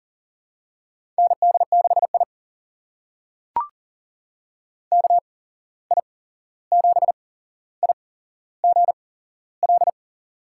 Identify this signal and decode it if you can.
Morse code, heard as a keyed tone, about 39 wpm, spaced as for 15 wpm, 700 Hz. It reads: ND6I KI7IGL
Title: USA Call Signs - 8x Character Spacing - 40wpm